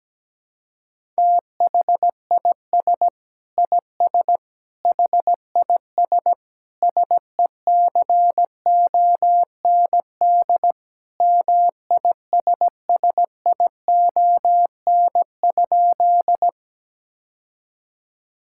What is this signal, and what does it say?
Morse code, heard as a keyed tone, about 17 wpm, 700 Hz. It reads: THIS IS HIS SECOND MISSION?
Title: Sentences from Top 1600 Words 17wpm